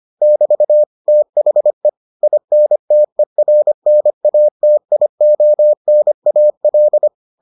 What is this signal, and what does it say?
Morse code, heard as a keyed tone, about 25 wpm, 600 Hz. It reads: <BT> THE INTERNATIONAL